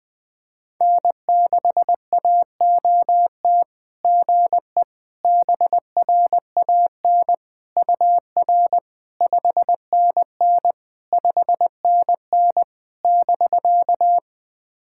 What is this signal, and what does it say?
Morse code, heard as a keyed tone, about 20 wpm, 690 Hz. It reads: N6AOT GE BRAN UR 5NN 5NN <BK>
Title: SOTA QSO Practice 20wpm